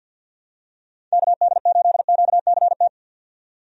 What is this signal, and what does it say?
Morse code, heard as a keyed tone, about 47 wpm, 700 Hz. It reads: KD8QCT
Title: USA Call Signs - 6 Characters - 50wpm